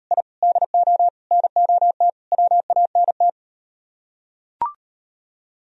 Morse code, heard as a keyed tone, about 38 wpm, 700 Hz. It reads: I DO NOT WANT